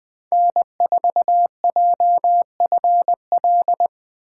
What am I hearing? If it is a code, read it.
Morse code, heard as a keyed tone, about 20 wpm, 700 Hz. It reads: N4JFL
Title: USA Call Signs - 5 Characters - 4x Character Spacing - 20wpm